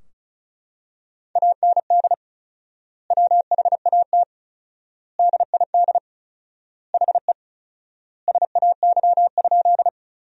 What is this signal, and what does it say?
Morse code, heard as a keyed tone, about 36 wpm, 700 Hz. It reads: AND WHAT DID HE SAY?
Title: 4x Word-Spacing - Sentences from Top 200 Words - 35wpm